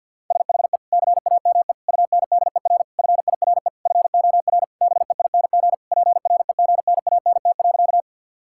Morse code, heard as a keyed tone, about 50 wpm, 700 Hz. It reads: SHE CAME UNDER FIRE FOR BEING PREGNANT.
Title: Sentences from Top 1000 Words - Review Part 6 - 50wpm